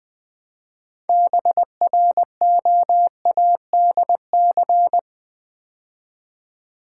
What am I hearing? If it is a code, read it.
Morse code, heard as a keyed tone, about 21 wpm, 700 Hz. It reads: BROADC
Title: Build A Word - 3 Letters At A Time - 9-Letter Words - 20wpm